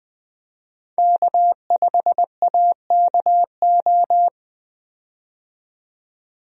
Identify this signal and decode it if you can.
Morse code, heard as a keyed tone, about 21 wpm, 700 Hz. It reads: K5AKO
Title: USA Call Signs - 5 Characters - 20wpm